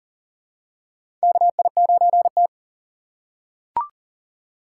Morse code, heard as a keyed tone, about 40 wpm, 700 Hz. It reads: KI9T
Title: USA Call Signs - 4 Characters - 6x Character Spacing - 40wpm